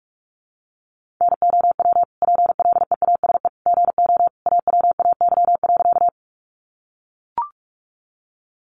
Morse code, heard as a keyed tone, about 45 wpm, 700 Hz. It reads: NOW PLEASE GO AWAY.
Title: Sentences from Top 1500 Words 45wpm